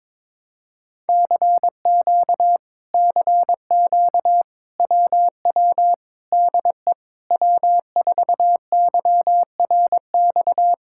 Morse code, heard as a keyed tone, about 22 wpm, 700 Hz. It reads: CQ CQ WW DE W4YRX
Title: Speed-Racing Common QSO Phrases Encore - 22wpm